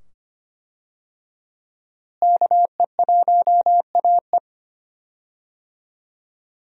Morse code, heard as a keyed tone, about 25 wpm, 700 Hz. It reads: KE1AE
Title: USA Call Signs - 5 Characters - Repeated 2x - 25wpm